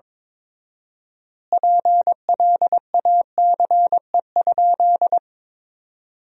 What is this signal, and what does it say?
Morse code, heard as a keyed tone, about 22 wpm, 700 Hz. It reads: PLACE?
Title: 4x Word-Spacing - Sentences from Top 200 Words - 22wpm